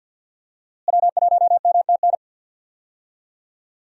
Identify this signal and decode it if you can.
Morse code, heard as a keyed tone, about 52 wpm, 700 Hz. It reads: W1MTN
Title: USA Call Signs - 5 Characters - Repeated 3x - 50wpm